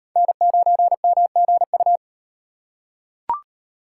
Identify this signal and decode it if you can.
Morse code, heard as a keyed tone, about 38 wpm, 700 Hz. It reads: N9MGU